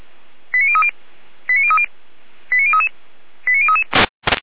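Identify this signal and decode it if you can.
Five-tone selective call (ZVEI), heard as quick runs of five tones, tones of about 70 ms. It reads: 89030 89030 89033 89033
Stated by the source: ZVEI five-tone selective call received on SDR, Feuerwehr Orlamünde.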